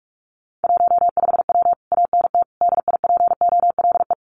Morse code, heard as a keyed tone, about 44 wpm, 700 Hz. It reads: 15W ANT DIPOLE
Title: Common QSO Phrases Encore 45wpm